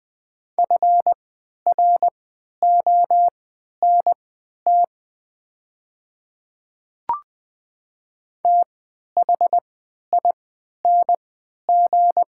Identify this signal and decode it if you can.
Morse code, heard as a keyed tone, about 20 wpm, spaced as for 11 wpm, 700 Hz. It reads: FRONT THING
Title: 3x Character Spacing - Top 500 Words - 20wpm